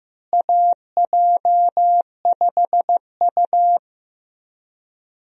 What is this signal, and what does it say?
Morse code, heard as a keyed tone, about 16 wpm, 700 Hz. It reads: AJ5U
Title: USA Call Signs - 4 Characters - Repeated 3x - 15wpm